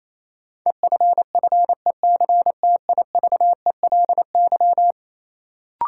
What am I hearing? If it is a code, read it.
Morse code, heard as a keyed tone, about 28 wpm, 700 Hz. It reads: EFFECTIVELY
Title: Sentences from Top 1400 Words - New Word Drill - 28wpm